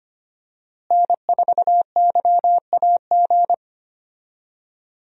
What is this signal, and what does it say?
Morse code, heard as a keyed tone, about 25 wpm, 700 Hz. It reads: N4YAG
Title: USA Call Signs - 5 Characters - Repeated 3x - 25wpm